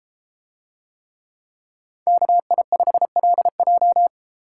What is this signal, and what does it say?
Morse code, heard as a keyed tone, about 32 wpm, 700 Hz. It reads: KI5LJ